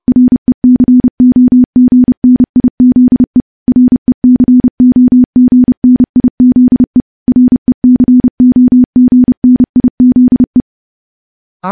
Morse code, heard as a keyed tone, about 28 wpm, 250 Hz. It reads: RECOGNIZE RECOGNIZE RECOGNIZE
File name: morse / morse-training-30WPM